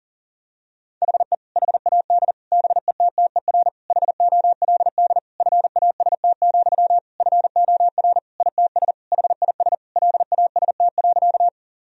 Morse code, heard as a keyed tone, about 40 wpm, 700 Hz. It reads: HE HAD BETTER HOLD FAST, FOR ITS HIS LAST.
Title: Sentences from Top 1000 Words - Review Part 2 - 40wpm